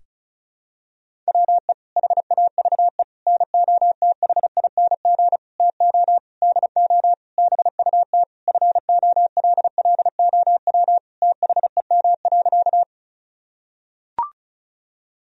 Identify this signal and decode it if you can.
Morse code, heard as a keyed tone, about 36 wpm, 700 Hz. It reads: WE HAVE NOTHING TO DO BUT FOLLOW THEM.
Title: Sentences from Top 1000 Words - Review Part 1 - 35wpm